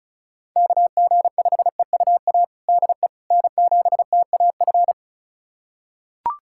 Morse code, heard as a keyed tone, about 36 wpm, 690 Hz. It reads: KG5EUA DE N7TAF
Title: USA Callsign DE Callsign 35wpm